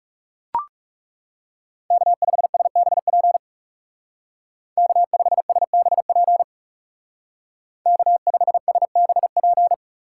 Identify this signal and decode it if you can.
Morse code, heard as a keyed tone, about 36 wpm, 700 Hz. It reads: K5SBP K5SBP K5SBP
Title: Speed Racing - USA Call Signs 30wpm